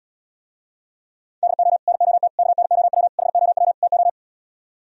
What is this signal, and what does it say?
Morse code, heard as a keyed tone, about 77 wpm, 700 Hz. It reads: DO NOT KNOW HOW TO